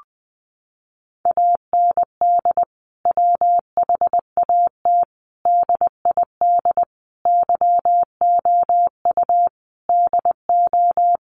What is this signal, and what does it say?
Morse code, heard as a keyed tone, about 20 wpm, 700 Hz. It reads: AND WHAT DID YOU DO
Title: Sentences from Top 100 Words 20wpm